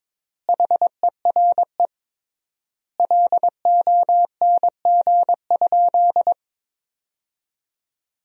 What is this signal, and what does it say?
Morse code, heard as a keyed tone, about 22 wpm, 700 Hz. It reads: HERE LONG?
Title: 3x Word-Spacing - Sentences from Top 200 Words - 22wpm